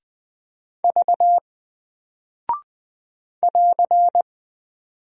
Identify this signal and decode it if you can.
Morse code, heard as a keyed tone, about 20 wpm, 700 Hz. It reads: V <AR>
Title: Single Character (Letter:Number:Punctuation:Prosign) 20wpm